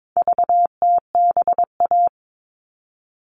Morse code, heard as a keyed tone, about 22 wpm, 700 Hz. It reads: VTBA
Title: All Cut Numbers - 4-Digit 22wpm